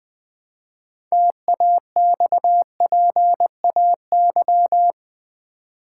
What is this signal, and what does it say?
Morse code, heard as a keyed tone, about 20 wpm, 700 Hz. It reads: TAXPAY
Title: Build A Word - 2 Letters At A Time - 8-Letter Words - 20wpm